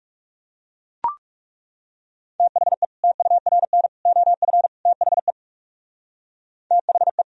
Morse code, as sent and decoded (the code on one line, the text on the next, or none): - .... / - ..- .-. -. / --- ..-. / - .... / - .... .
THE TURN OF THE THE